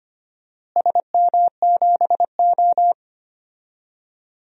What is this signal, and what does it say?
Morse code, heard as a keyed tone, about 25 wpm, 700 Hz. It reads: SM7O
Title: DXCC-100 Call Signs - 4 Characters - Repeated 3x - 25wpm